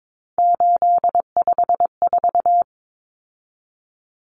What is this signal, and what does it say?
Morse code, heard as a keyed tone, about 22 wpm, 700 Hz. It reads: 854
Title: Common Cut Numbers - 3-Digit 22wpm